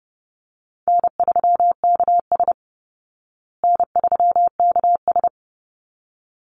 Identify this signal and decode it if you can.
Morse code, heard as a keyed tone, about 28 wpm, 700 Hz. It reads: N3KS N3KS
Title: Call Signs Repeated 3x - 30wpm